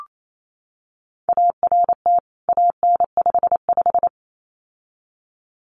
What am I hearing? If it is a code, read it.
Morse code, heard as a keyed tone, about 28 wpm, 700 Hz. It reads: ART AN55
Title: CWT Exchange 28wpm